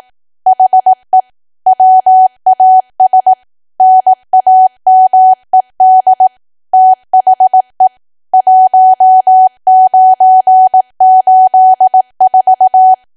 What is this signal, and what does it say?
Morse code, heard as a keyed tone, about 18 wpm, 750 Hz. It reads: HE WAS NAMED THE 1984
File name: Archive / 18 WPM / 18WPM